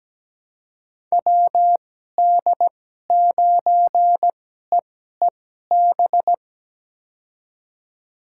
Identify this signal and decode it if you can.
Morse code, heard as a keyed tone, about 17 wpm, spaced as for 12 wpm, 700 Hz. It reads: WD9EEB
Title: USA Call Signs - 6 Characters - 2x Character Spacing - 17wpm